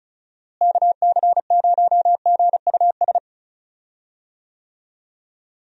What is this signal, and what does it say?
Morse code, heard as a keyed tone, about 34 wpm, 700 Hz. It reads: KC0GUS
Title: USA Call Signs - 6 Characters - Repeated 2x - 35wpm